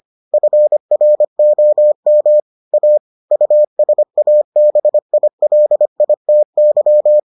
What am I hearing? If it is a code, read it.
Morse code, heard as a keyed tone, about 25 wpm, 600 Hz. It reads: FROM A USABILITY